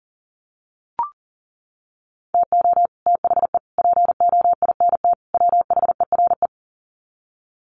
Morse code, heard as a keyed tone, about 40 wpm, 700 Hz. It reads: TO THE POINT WHERE
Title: Sets of 4 Words 40wpm